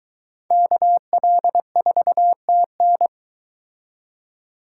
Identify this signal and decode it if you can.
Morse code, heard as a keyed tone, about 22 wpm, 700 Hz. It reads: KL4TN